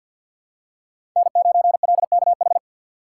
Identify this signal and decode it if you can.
Morse code, heard as a keyed tone, about 47 wpm, 700 Hz. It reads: N9LKH
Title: USA Call Signs - 5 Characters - 4x Character Spacing - 50wpm